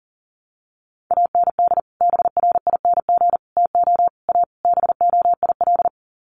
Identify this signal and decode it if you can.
Morse code, heard as a keyed tone, about 40 wpm, 700 Hz. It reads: AND BRING TO A BOIL
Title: Sets of 5 Words Encore 40wpm